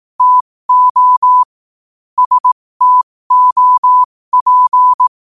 Morse code, heard as a keyed tone, about 18 wpm, spaced as for 15 wpm, 1000 Hz. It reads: TO STOP
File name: qotdfiles / 15wpm